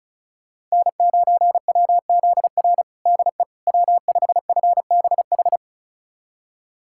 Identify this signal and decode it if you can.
Morse code, heard as a keyed tone, about 36 wpm, 700 Hz. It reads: N9WZR DE W5FBH
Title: USA Callsign DE Callsign - 5 Characters - 35wpm